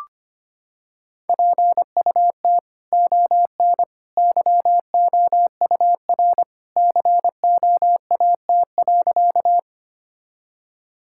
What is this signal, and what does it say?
Morse code, heard as a keyed tone, about 25 wpm, 700 Hz. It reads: PUT ON YOUR COAT.
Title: Sentences from Top 1600 Words 25wpm